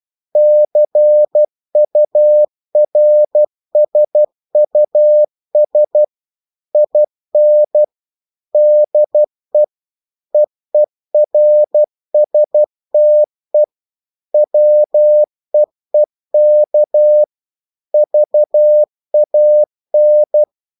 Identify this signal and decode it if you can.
Morse code, heard as a keyed tone, about 12 wpm, 600 Hz. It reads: CURSUS IN DE EERSTE WEEK VAN